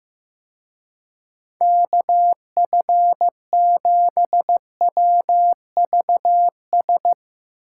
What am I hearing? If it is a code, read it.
Morse code, heard as a keyed tone, about 16 wpm, 700 Hz. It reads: KF7WVS
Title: Speed Racing - USA 2x3 Call Signs 15wpm